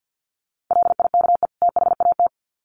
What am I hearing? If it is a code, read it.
Morse code, heard as a keyed tone, about 48 wpm, 700 Hz. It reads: LIKE THAT
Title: Sets of 2 Words 50wpm